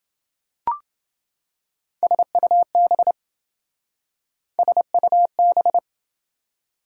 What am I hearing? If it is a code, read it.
Morse code, heard as a keyed tone, about 27 wpm, 700 Hz. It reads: SUB SUB